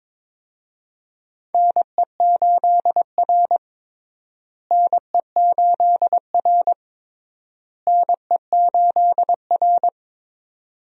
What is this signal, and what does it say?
Morse code, heard as a keyed tone, about 22 wpm, 700 Hz. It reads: NE8R NE8R NE8R
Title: USA Call Signs - 4 Characters - Repeated 3x - 22wpm